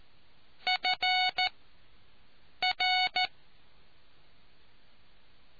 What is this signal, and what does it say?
Morse code, heard as a keyed tone, about 14 wpm, spaced as for 6 wpm, 740 Hz. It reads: FR